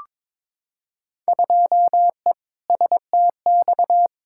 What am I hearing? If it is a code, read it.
Morse code, heard as a keyed tone, about 22 wpm, 700 Hz. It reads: 2E STX
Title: Field Day Exchanges Encore - 22wpm